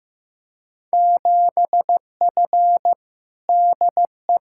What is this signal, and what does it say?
Morse code, heard as a keyed tone, about 15 wpm, 700 Hz. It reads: 7F DE
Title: Field Day Exchanges Encore - 15wpm